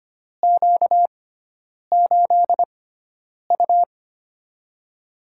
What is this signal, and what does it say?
Morse code, heard as a keyed tone, about 25 wpm, spaced as for 9 wpm, 700 Hz. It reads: Q8U
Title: USA Call Signs - 4 Characters - 6x Character Spacing - 25wpm